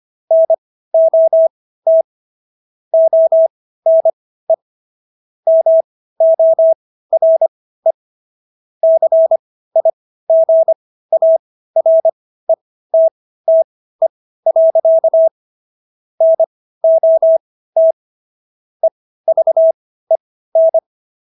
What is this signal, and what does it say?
Morse code, heard as a keyed tone, about 25 wpm, spaced as for 15 wpm, 650 Hz. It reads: NOT ONE MORE CIGARETTE. NOT EVEN